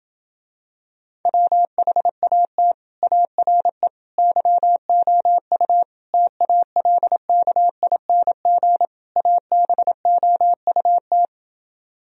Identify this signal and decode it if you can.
Morse code, heard as a keyed tone, about 27 wpm, 700 Hz. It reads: WHAT ARE YOU TALKING ABOUT